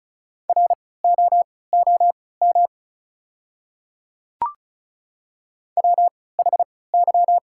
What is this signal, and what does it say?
Morse code, heard as a keyed tone, about 36 wpm, spaced as for 20 wpm, 690 Hz. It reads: ROOM WHY
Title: 3x Character Spacing - Top 300 Words - 35wpm